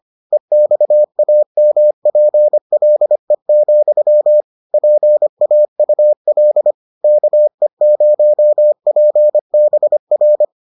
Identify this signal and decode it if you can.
Morse code, heard as a keyed tone, about 25 wpm, 600 Hz. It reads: EXAMPLE, PAUL KE0PBR